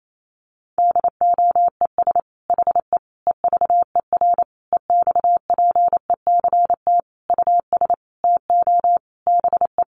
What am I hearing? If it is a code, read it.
Morse code, heard as a keyed tone, about 28 wpm, 700 Hz. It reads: DOES HE EVER EXPECT US TO BE
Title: Sentences from Top 1100 Words 28wpm